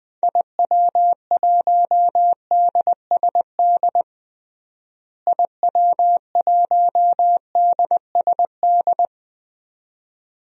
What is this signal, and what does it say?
Morse code, heard as a keyed tone, about 20 wpm, 700 Hz. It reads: IW1DSD IW1DSD
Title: DXCC-100 Call Signs - 6 Characters - Repeated 2x - 20wpm